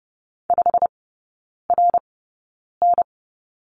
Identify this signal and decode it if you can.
Morse code, heard as a keyed tone, about 30 wpm, spaced as for 9 wpm, 700 Hz. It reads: 5RN
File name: course / 30:10wpm / 087 Lesson 15e - Number 5 - Three Characters 30wpm